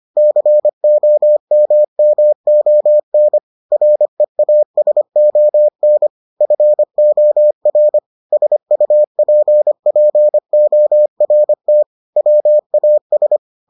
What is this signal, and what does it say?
Morse code, heard as a keyed tone, about 25 wpm, 600 Hz. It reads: COMMON REASON FOR SUPPORT WAS